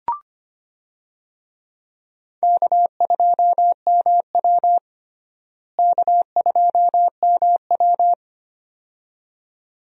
Morse code, heard as a keyed tone, about 25 wpm, 700 Hz. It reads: K2MW K2MW
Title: USA Call Signs - 4 Characters - Repeated 2x - 25wpm